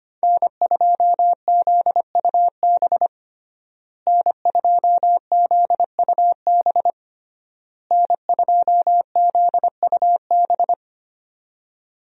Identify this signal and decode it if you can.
Morse code, heard as a keyed tone, about 25 wpm, 700 Hz. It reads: N2ZUB N2ZUB N2ZUB
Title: USA Call Signs - 5 Characters - Repeated 3x - 25wpm